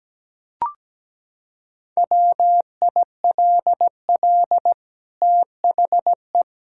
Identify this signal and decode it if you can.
Morse code, heard as a keyed tone, about 17 wpm, 700 Hz. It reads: WILL THE
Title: Sentences from Top 1000 Words - Review Part 6 - 17wpm